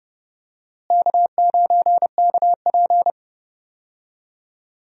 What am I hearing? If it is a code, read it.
Morse code, heard as a keyed tone, about 30 wpm, 700 Hz. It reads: K9KP
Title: USA Call Signs - 4 Characters - Repeated 2x - 30wpm